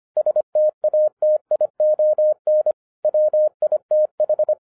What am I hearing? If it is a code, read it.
Morse code, heard as a keyed tone, about 25 wpm, 610 Hz. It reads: STATION WITH